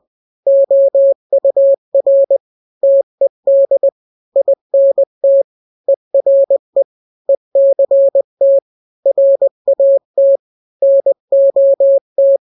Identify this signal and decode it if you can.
Morse code, heard as a keyed tone, about 20 wpm, 550 Hz. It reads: OUR TED INT ERE ECT RAT NOT
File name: n-grams / 3-grams 20 18 WPM